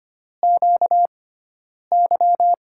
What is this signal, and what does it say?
Morse code, heard as a keyed tone, about 25 wpm, spaced as for 9 wpm, 700 Hz. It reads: QY
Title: USA Call Signs - 6 Characters - 6x Character Spacing - 25wpm